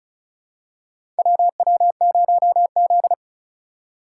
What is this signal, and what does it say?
Morse code, heard as a keyed tone, about 35 wpm, 700 Hz. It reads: WW0Z
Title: USA Call Signs - 3x Character Spacing - 35wpm